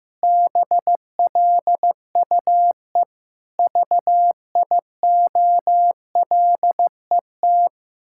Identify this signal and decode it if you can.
Morse code, heard as a keyed tone, about 14 wpm, 700 Hz. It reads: BLUE VIOLET
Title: Pangram Challenge v2 - 15wpm